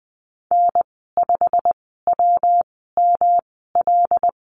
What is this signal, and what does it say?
Morse code, heard as a keyed tone, about 19 wpm, spaced as for 14 wpm, 700 Hz. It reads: N5WML
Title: 179 Lesson 29g - Slash - Call Signs 20wpm